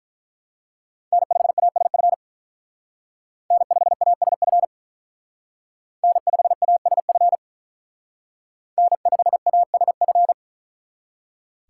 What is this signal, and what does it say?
Morse code, heard as a keyed tone, about 43 wpm, 700 Hz. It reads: N5ASF N5ASF N5ASF N5ASF
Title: Speed Racing - USA 1x3 Call Signs 35wpm